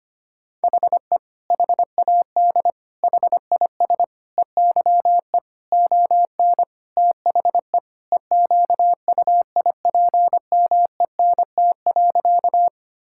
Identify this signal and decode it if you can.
Morse code, heard as a keyed tone, about 25 wpm, 700 Hz. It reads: HE HAD HIS EYE ON THE EQUIPMENT.
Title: Sentences from Top 1000 Words - Review Part 5 - 25wpm